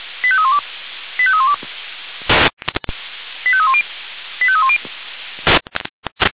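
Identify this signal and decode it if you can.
Five-tone selective call (ZVEI), heard as quick runs of five tones, tones of about 70 ms. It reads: 85312 85312 85310 85310
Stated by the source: ZVEI five-tone selective call, SDR reception.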